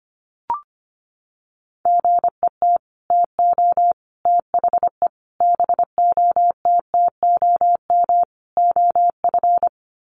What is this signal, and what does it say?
Morse code, heard as a keyed tone, about 25 wpm, 700 Hz. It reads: GET TO THE BOTTOM OF
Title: Sets of 5 Words Encore 25wpm